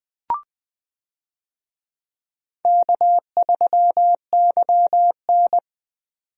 Morse code, heard as a keyed tone, about 20 wpm, 700 Hz. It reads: K3YN
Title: USA Call Signs - 4 Characters - 20wpm